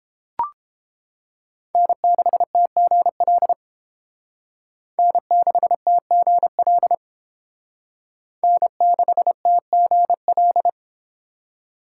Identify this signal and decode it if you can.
Morse code, heard as a keyed tone, about 30 wpm, 700 Hz. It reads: N6TGL N6TGL N6TGL